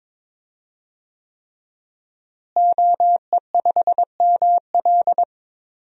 Morse code, heard as a keyed tone, about 22 wpm, 700 Hz. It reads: OE5ML
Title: DXCC-100 Call Signs - 5 Characters - Repeated 3x - 22wpm